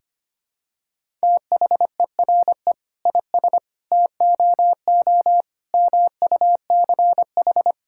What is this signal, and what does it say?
Morse code, heard as a keyed tone, about 25 wpm, 700 Hz. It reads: THERE IS TOO MUCH
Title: Sentences from Top 1000 Words - Review Part 5 - 25wpm